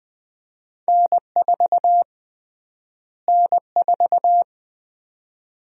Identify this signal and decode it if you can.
Morse code, heard as a keyed tone, about 20 wpm, 700 Hz. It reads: N4 N4
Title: USA Call Sign Prefixes - Repeated 2x - 20wpm